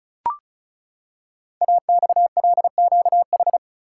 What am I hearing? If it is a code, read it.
Morse code, heard as a keyed tone, about 36 wpm, 700 Hz. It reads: AXLQH
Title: Code Groups - 5 Letters - 35wpm